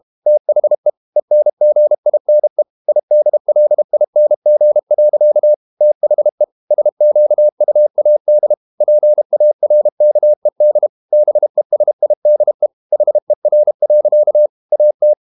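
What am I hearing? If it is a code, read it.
Morse code, heard as a keyed tone, about 32 wpm, 600 Hz. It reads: THE ENGINE IDLING. THE SQUAD PARKED BESIDE HER. AT
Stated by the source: Morse code practice at 32 wpm from CW med Gnister, reading Ken Follett's novel Never in English.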